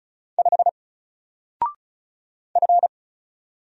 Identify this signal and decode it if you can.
Morse code, heard as a keyed tone, about 33 wpm, 700 Hz. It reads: 5 F
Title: Single Letter-Number Alt 35wpm